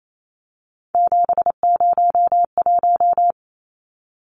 Morse code, heard as a keyed tone, about 28 wpm, 700 Hz. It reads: 701
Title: Numbers - 3-Digit 28wpm